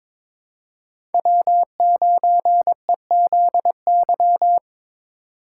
Morse code, heard as a keyed tone, about 22 wpm, 700 Hz. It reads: W9EZY
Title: USA Call Signs - 5 Characters - 4x Character Spacing - 22wpm